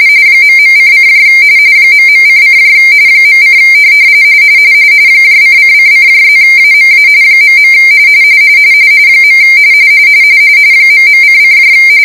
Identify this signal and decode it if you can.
RTTY, about 45.45 baud. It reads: ..:: ..:: .: ::::+AMA .. ..::